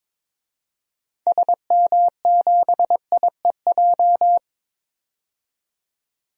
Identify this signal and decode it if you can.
Morse code, heard as a keyed tone, about 22 wpm, 700 Hz. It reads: SM7IEJ
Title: DXCC-100 Call Signs - 6 Characters - 22wpm